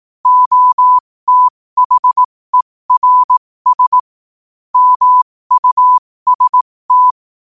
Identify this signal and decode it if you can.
Morse code, heard as a keyed tone, about 18 wpm, spaced as for 15 wpm, 1000 Hz. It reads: OTHERS MUST
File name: qotdfiles / 15wpm